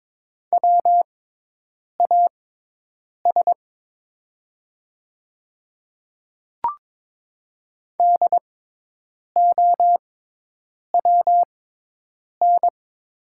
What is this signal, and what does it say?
Morse code, heard as a keyed tone, about 22 wpm, spaced as for 8 wpm, 700 Hz. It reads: WAS DOWN
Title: 6x Character Spacing - Top 100 Words - 22wpm